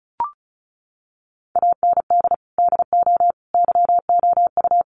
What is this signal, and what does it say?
Morse code, heard as a keyed tone, about 34 wpm, 700 Hz. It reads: AND DO YOU
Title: Sentences from Top 100 Words 35wpm